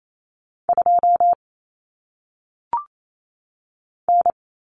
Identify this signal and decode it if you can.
Morse code, heard as a keyed tone, about 28 wpm, 700 Hz. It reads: 2 N
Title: Common Cut Numbers - 1-Digit 28wpm